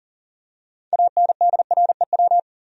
Morse code, heard as a keyed tone, about 39 wpm, 700 Hz. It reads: ANDREW
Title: USA Common Names - 40wpm